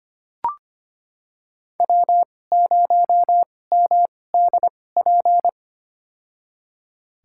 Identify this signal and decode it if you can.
Morse code, heard as a keyed tone, about 25 wpm, spaced as for 18 wpm, 700 Hz. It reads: W0MDP
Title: USA Call Signs - 5 Characters - 2x Character Spacing - 25wpm